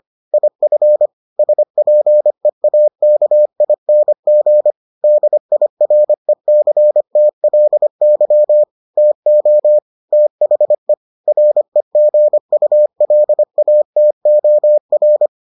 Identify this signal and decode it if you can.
Morse code, heard as a keyed tone, about 25 wpm, 600 Hz. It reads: IF SPEAKING DIRECTLY TO THE REGULATOR